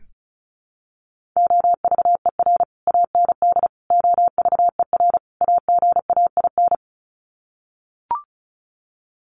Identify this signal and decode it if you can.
Morse code, heard as a keyed tone, about 34 wpm, 700 Hz. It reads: OVER AND OVER AGAIN
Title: Sets of 4 Words 35wpm